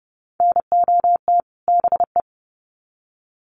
Morse code, heard as a keyed tone, about 30 wpm, 700 Hz. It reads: NOT BE
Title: Sets of 2 Words 30wpm